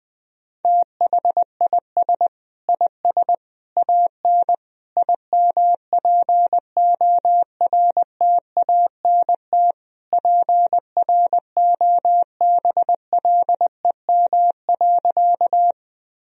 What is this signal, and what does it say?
Morse code, heard as a keyed tone, about 20 wpm, 700 Hz. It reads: THIS IS AN IMPORTANT PROBLEM.
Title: Sentences from Top 1000 Words - Review Part 2 - 20wpm